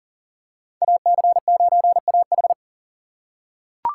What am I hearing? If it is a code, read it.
Morse code, heard as a keyed tone, about 40 wpm, 700 Hz. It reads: AC9AH
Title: USA Call Signs - 5 Characters - 4x Character Spacing - 40wpm